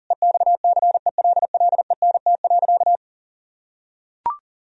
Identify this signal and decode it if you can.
Morse code, heard as a keyed tone, about 40 wpm, 700 Hz. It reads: EXCELLENT.
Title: Sentences from Top 1000 Words - Review Part 4 - 40wpm